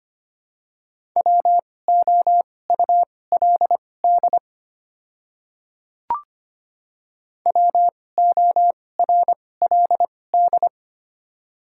Morse code, heard as a keyed tone, about 25 wpm, spaced as for 18 wpm, 700 Hz. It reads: WOULD WORLD
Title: 2x Character Spacing - Top 200 Words - 25wpm